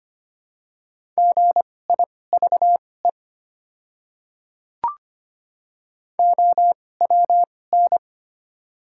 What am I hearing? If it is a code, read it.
Morse code, heard as a keyed tone, about 25 wpm, spaced as for 18 wpm, 700 Hz. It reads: GIVE OWN
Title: 2x Character Spacing - Top 300 Words - 25wpm